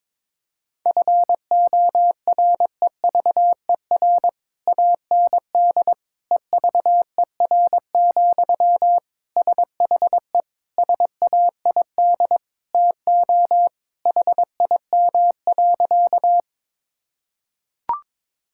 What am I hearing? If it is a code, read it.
Morse code, heard as a keyed tone, about 22 wpm, 700 Hz. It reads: FOREVER AND EVER, SHE SAID TO HIM.
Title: Sentences from Top 1600 Words 22wpm